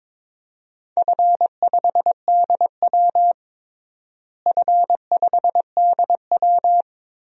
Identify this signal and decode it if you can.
Morse code, heard as a keyed tone, about 22 wpm, 700 Hz. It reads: F5DW F5DW
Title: DXCC-100 Call Signs - 4 Characters - Repeated 3x - 22wpm